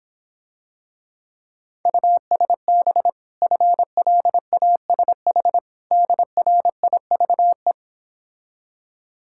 Morse code, heard as a keyed tone, about 26 wpm, 700 Hz. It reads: USB FLASH DRIVE